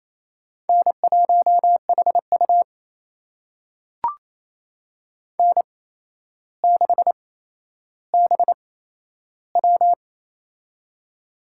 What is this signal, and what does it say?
Morse code, heard as a keyed tone, about 28 wpm, spaced as for 12 wpm, 700 Hz. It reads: N1HU N6BW
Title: USA Call Signs - 4 Characters - 8x Character Spacing - 28wpm